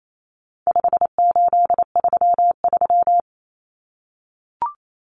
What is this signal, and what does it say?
Morse code, heard as a keyed tone, about 28 wpm, 700 Hz. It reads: 5833
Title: Numbers - 4-Digit 28wpm